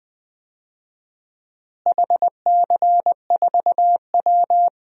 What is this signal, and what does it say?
Morse code, heard as a keyed tone, about 20 wpm, 700 Hz. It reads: HC4W
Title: DXCC-100 Call Signs - 4 Characters - Repeated 3x - 20wpm